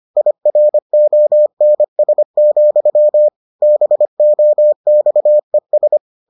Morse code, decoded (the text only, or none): IRONS, BOXES